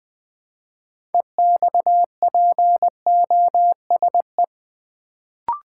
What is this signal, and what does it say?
Morse code, heard as a keyed tone, about 20 wpm, 700 Hz. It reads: EXPOSE
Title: Build A Word - 2 Letters At A Time - 6-Letter Words - 20wpm